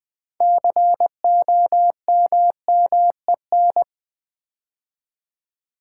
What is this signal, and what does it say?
Morse code, heard as a keyed tone, about 20 wpm, 700 Hz. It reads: COMMEN
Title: Build A Word - 3 Letters At A Time - 8-Letter Words - 20wpm